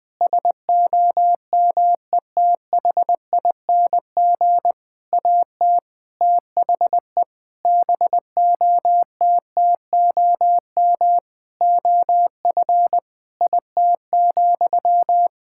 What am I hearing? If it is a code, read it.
Morse code, heard as a keyed tone, about 20 wpm, 700 Hz. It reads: SOMETHING AT THE BOTTOM OF IT,
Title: Sentences from Top 1000 Words - Review Part 6 - 20wpm